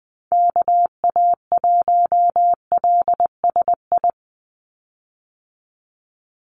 Morse code, heard as a keyed tone, about 19 wpm, 700 Hz. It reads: KA1LSI
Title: USA Call Signs 20wpm